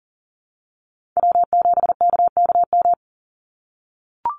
Morse code, heard as a keyed tone, about 40 wpm, 700 Hz. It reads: W7KKM
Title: NA SOTA Chaser Callsigns 40wpm